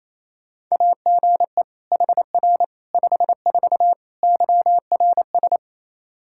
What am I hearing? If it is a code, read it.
Morse code, heard as a keyed tone, about 28 wpm, 700 Hz. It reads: AGE HR 54 YRS
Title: Practice QSO - Exchange 3 - More Info - 28wpm